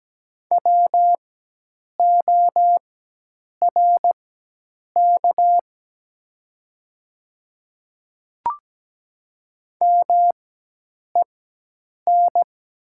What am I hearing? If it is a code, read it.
Morse code, heard as a keyed tone, about 17 wpm, spaced as for 8 wpm, 700 Hz. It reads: WORK MEN
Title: 4x Character Spacing - Top 200 Words - 17wpm